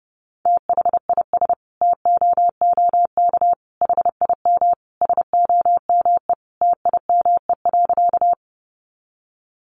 Morse code, heard as a keyed tone, about 28 wpm, 700 Hz. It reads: THIS TOOK HIM SOME TIME.
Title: Sentences from Top 300 Words 30wpm